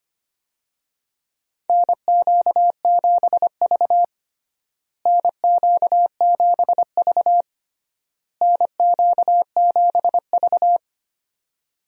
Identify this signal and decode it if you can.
Morse code, heard as a keyed tone, about 25 wpm, 700 Hz. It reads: NQ7V NQ7V NQ7V
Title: USA Call Signs - 4 Characters - Repeated 3x - 25wpm